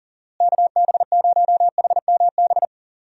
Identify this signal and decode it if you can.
Morse code, heard as a keyed tone, about 40 wpm, 700 Hz. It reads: KB0HMB
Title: USA Call Signs - 6 Characters - Repeated 2x - 40wpm